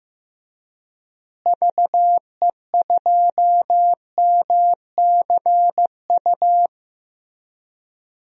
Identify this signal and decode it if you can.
Morse code, heard as a keyed tone, about 15 wpm, 700 Hz. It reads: VE2MCU
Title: DXCC-100 Call Signs - 6 Characters - Repeated 3x - 15wpm